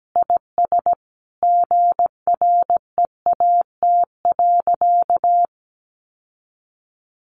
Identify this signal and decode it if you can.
Morse code, heard as a keyed tone, about 17 wpm, 700 Hz. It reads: IS GREAT.
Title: Sentences from Top 600 Words 17wpm